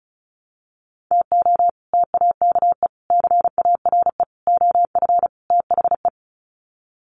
Morse code, heard as a keyed tone, about 34 wpm, 700 Hz. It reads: TO TAKE CARE OF THE
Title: Sets of 5 Words Encore 35wpm